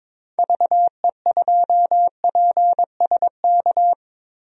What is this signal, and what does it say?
Morse code, heard as a keyed tone, about 22 wpm, 700 Hz. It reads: VE2PSK
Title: DXCC-100 Call Signs - 6 Characters - 22wpm